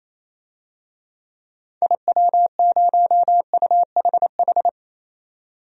Morse code, heard as a keyed tone, about 28 wpm, 700 Hz. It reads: IW0UHH
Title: DXCC-100 Call Signs - 6 Characters - 28wpm